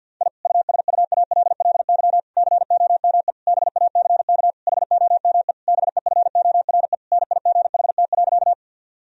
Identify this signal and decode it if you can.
Morse code, heard as a keyed tone, about 50 wpm, 700 Hz. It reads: I USUALLY COME BACK HOME BEFORE NIGHT.
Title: Sentences from Top 1000 Words - Review Part 3 - 50wpm